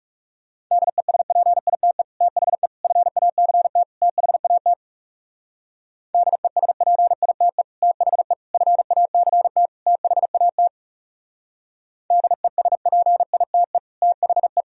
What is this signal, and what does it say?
Morse code, heard as a keyed tone, about 37 wpm, 700 Hz. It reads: DESPITE THE FACT THAT DESPITE THE FACT THAT DESPITE THE
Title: Speed-Racing - Sets of 4 Words 30wpm